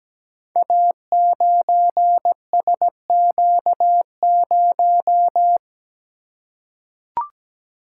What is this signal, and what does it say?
Morse code, heard as a keyed tone, about 17 wpm, 700 Hz. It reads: A9SQ0
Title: Code Groups - 5 Letter:Numbers - 17wpm